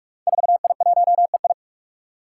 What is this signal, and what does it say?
Morse code, heard as a keyed tone, about 46 wpm, 700 Hz. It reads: 4I1EI
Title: Five Challenging Letter-Number Combinations 45wpm